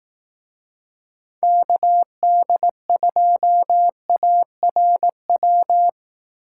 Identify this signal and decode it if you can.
Morse code, heard as a keyed tone, about 18 wpm, 700 Hz. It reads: KD2ARW
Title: Speed Racing - USA 2x3 Call Signs 15wpm